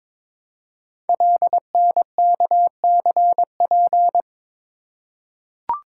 Morse code, heard as a keyed tone, about 22 wpm, 700 Hz. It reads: LNKCP
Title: Code Groups - 5 Letter:Numbers - 22wpm